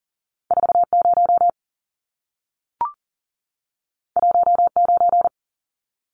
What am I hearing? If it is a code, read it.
Morse code, heard as a keyed tone, about 40 wpm, 700 Hz. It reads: 40 19
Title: Numbers - 2-Digit 40wpm